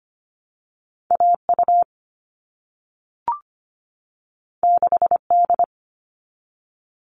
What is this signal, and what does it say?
Morse code, heard as a keyed tone, about 25 wpm, 700 Hz. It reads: AU 6D
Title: All Cut Numbers - 2-Digit 25wpm